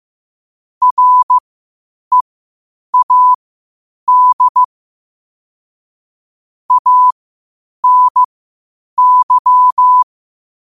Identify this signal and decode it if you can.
Morse code, heard as a keyed tone, about 15 wpm, spaced as for 8 wpm, 1000 Hz. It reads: READ ANY
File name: qotdfiles / 8wpm